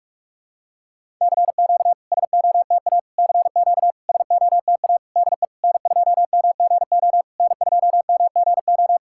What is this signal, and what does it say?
Morse code, heard as a keyed tone, about 44 wpm, 690 Hz. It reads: CQ SOTA CQ SOTA DE N2MGO N2MGO
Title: SOTA QSO Practice 45wpm